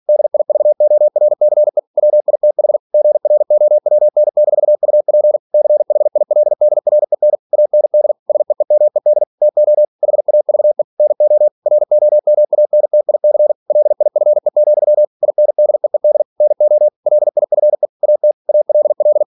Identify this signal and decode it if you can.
Morse code, heard as a keyed tone, about 46 wpm, 600 Hz. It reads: DIVORCE WITH GROWN-UP CHILDREN AND SEEMED TO HAVE NO ROMANTIC LIFE, INDEED NO LIFE AT ALL
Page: Never 47wpm